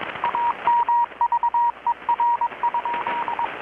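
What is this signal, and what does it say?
Morse code, heard as a keyed tone, about 22 wpm, 950 Hz, with radio noise. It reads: AMVER?
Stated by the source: Morse from KPH's Night of Nights XX maritime radiotelegraphy event, received in Chiapas, Mexico about 3,800 km away on a Kenwood R-600 with a grounded 10 m long-wire antenna.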